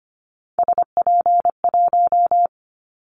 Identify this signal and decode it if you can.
Morse code, heard as a keyed tone, about 25 wpm, 700 Hz. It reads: SP1
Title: Callsign Prefixes for Top 25 DXCC - 25wpm